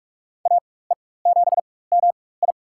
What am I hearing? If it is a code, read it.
Morse code, heard as a keyed tone, about 46 wpm, spaced as for 21 wpm, 700 Hz. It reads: AE7MI
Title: USA Call Signs - 5 Characters - 4x Character Spacing - 45wpm